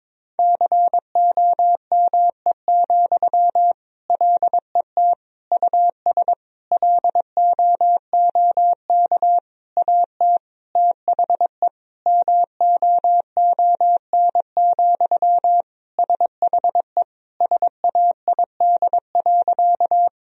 Morse code, heard as a keyed tone, about 22 wpm, 700 Hz. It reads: COME, LET US LOOK AT THE MOON, SHE SAID.
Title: Sentences from Top 1000 Words - Review Part 3 - 22wpm